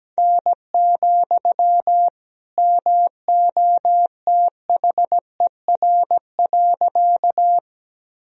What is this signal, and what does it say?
Morse code, heard as a keyed tone, about 17 wpm, 700 Hz. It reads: N, MOTHER.
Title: Sentences from Top 200 Words and Abbreviations 17wpm